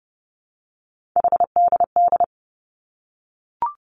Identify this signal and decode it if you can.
Morse code, heard as a keyed tone, about 30 wpm, 700 Hz. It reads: HDD
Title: Three Challenging Letter-Number Combinations 30wpm